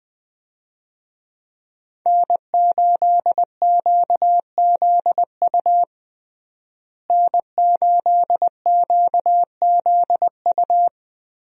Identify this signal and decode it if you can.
Morse code, heard as a keyed tone, about 20 wpm, 700 Hz. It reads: N8QZU N8QZU
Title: USA Call Signs - 5 Characters - Repeated 3x - 20wpm